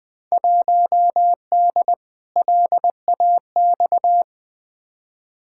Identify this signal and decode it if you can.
Morse code, heard as a keyed tone, about 20 wpm, 700 Hz. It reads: 1D LAX
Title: Field Day Exchanges - 20wpm